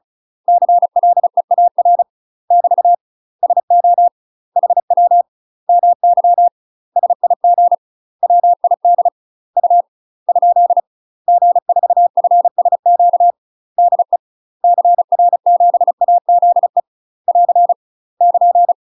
Morse code, heard as a keyed tone, about 34 wpm, 700 Hz. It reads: CLEAR <BT> SO HW MY SIG WID U ? G4FSQ DE CR7AZE <AR> <KN>
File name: UK MASTER QSOs TRAINING / 35WPM / UK MASTER QSOs 0016